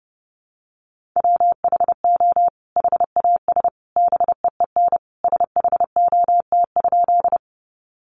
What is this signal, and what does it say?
Morse code, heard as a keyed tone, about 30 wpm, 700 Hz. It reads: WHO HAS BEEN SHOT?
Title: Sentences from Top 1100 Words 30wpm